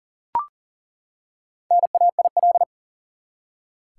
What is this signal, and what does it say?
Morse code, heard as a keyed tone, about 40 wpm, 700 Hz. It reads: NAIL
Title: Four Letter Words - 40wpm